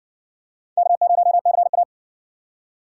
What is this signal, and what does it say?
Morse code, heard as a keyed tone, about 60 wpm, 700 Hz. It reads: K0CA